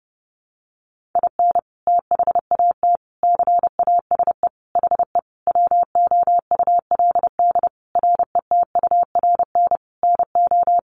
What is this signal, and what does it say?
Morse code, heard as a keyed tone, about 28 wpm, 700 Hz. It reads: IN THAT CASE HE WOULD RETURN NO
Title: Sentences from Top 800 Words 30wpm